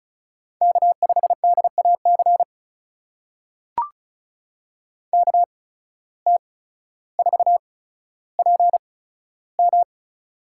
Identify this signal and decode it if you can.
Morse code, heard as a keyed tone, about 36 wpm, spaced as for 15 wpm, 700 Hz. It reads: K5DAC KT4PM
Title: USA Call Signs - 5 Characters - 8x Character Spacing - 35wpm